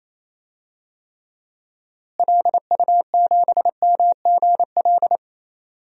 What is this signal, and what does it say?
Morse code, heard as a keyed tone, about 28 wpm, 700 Hz. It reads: LU7MGL
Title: DXCC-100 Call Signs - 6 Characters - Repeated 2x - 28wpm